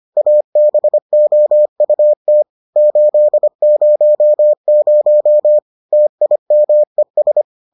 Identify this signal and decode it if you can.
Morse code, heard as a keyed tone, about 25 wpm, 600 Hz. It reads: ABOUT 800 TIMES